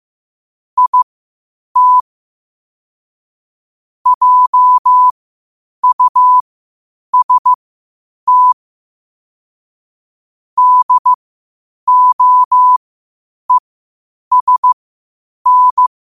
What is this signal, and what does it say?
Morse code, heard as a keyed tone, about 15 wpm, spaced as for 8 wpm, 1000 Hz. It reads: IT JUST DOESN
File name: qotdfiles / 8wpm